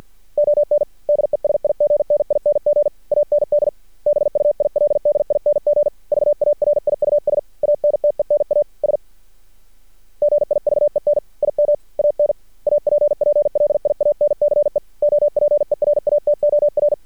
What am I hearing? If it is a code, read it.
Morse code, heard as a keyed tone, about 48 wpm, 590 Hz. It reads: ON DESIGNING AND BUILDING VARIUS ANTENAS GIVEN IM AN APPLIANCE OPERATOR
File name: archived / CRSnet